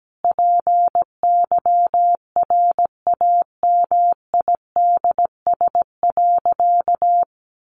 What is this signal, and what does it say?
Morse code, heard as a keyed tone, about 17 wpm, 700 Hz. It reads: PYRAMIDS.
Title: A Sound Of Thunder 17wpm